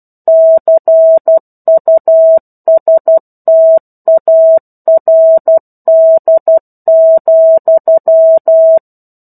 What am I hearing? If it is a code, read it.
Morse code, heard as a keyed tone, about 12 wpm, 650 Hz. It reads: CUSTARD,
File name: 01-Chapter-01 12 WPM